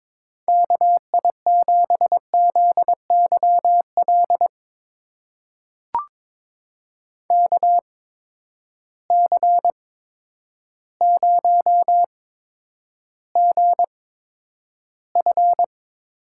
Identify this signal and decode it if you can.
Morse code, heard as a keyed tone, about 22 wpm, spaced as for 10 wpm, 700 Hz. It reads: KI7ZYL KC0GF
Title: USA Call Signs - 6 Characters - 8x Character Spacing - 22wpm